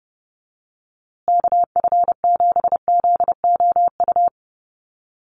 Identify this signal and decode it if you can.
Morse code, heard as a keyed tone, about 30 wpm, 700 Hz. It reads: KF7ZOU
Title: USA Call Signs Repeated 3x 30wpm